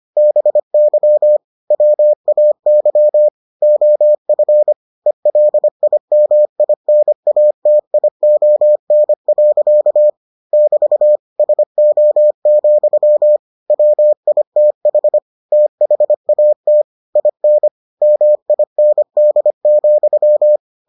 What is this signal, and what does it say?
Morse code, heard as a keyed tone, about 25 wpm, 600 Hz. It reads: BY WAY OF ELIMINATION. <BT> SO, WITH THAT IN MIND,